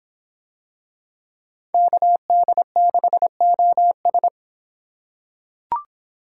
Morse code, heard as a keyed tone, about 25 wpm, 700 Hz. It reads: KD6OS